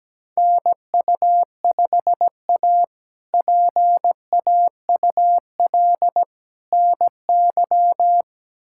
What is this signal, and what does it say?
Morse code, heard as a keyed tone, about 17 wpm, 700 Hz. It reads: NU5A PAUL NY
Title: K1USN SST QSO Practice - 17wpm